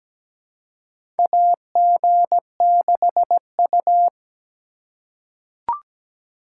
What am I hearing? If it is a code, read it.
Morse code, heard as a keyed tone, about 17 wpm, 700 Hz. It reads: AG6U
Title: USA Call Signs - 4 Characters - 4x Character Spacing - 17wpm